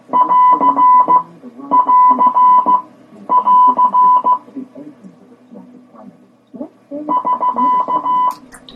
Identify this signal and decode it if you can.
Morse code, heard as a keyed tone, about 15 wpm, spaced as for 10 wpm, 1000 Hz. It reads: <AR><AR><AR> <SK>